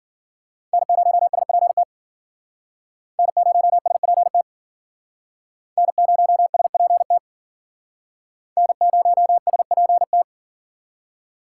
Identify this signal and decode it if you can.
Morse code, heard as a keyed tone, about 52 wpm, 700 Hz. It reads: N0SPT N0SPT N0SPT N0SPT